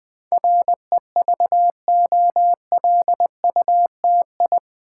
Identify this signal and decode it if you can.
Morse code, heard as a keyed tone, about 20 wpm, 700 Hz. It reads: REVOLUTI
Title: Build A Word - 4 Letters At A Time - 10-Letter Words - 20wpm